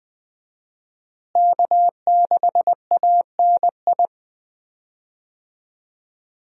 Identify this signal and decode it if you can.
Morse code, heard as a keyed tone, about 20 wpm, 700 Hz. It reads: K6ANI
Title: USA Call Signs - 5 Characters - Repeated 3x - 20wpm